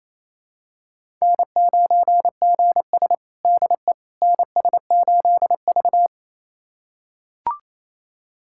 Morse code, heard as a keyed tone, about 28 wpm, 700 Hz. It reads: N9GS DE NS8V
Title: USA Callsign DE Callsign - 4 Characters - 28wpm